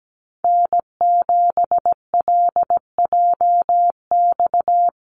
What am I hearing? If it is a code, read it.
Morse code, heard as a keyed tone, about 17 wpm, 700 Hz. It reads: N7LJX
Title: USA Call Signs Repeated 2x 17wpm